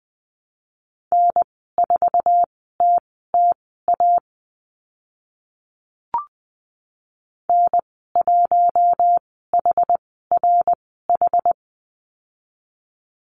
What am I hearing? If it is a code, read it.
Morse code, heard as a keyed tone, about 20 wpm, spaced as for 14 wpm, 700 Hz. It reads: N4TTA N1HRH
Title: 062 Lesson 11g - Letter h - Call Signs 20wpm